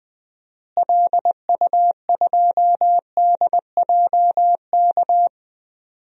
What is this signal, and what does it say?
Morse code, heard as a keyed tone, about 21 wpm, 700 Hz. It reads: LU2DJK
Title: DXCC-100 Call Signs - 6 Characters - Repeated 3x - 20wpm